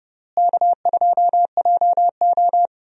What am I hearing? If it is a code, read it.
Morse code, heard as a keyed tone, about 30 wpm, 700 Hz. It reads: K2JO
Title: USA Call Signs - 4 Characters - Repeated 2x - 30wpm